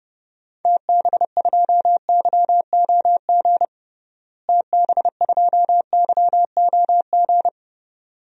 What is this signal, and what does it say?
Morse code, heard as a keyed tone, about 32 wpm, 700 Hz. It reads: TB2YOG TB2YOG
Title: DXCC-100 Call Signs - 6 Characters - Repeated 2x - 30wpm